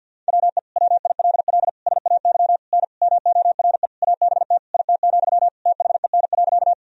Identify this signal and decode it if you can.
Morse code, heard as a keyed tone, about 50 wpm, 700 Hz. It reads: WE WILL SAY N MORE ABT IT, THEN.
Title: Sentences from Top 200 Words and Abbreviations 50wpm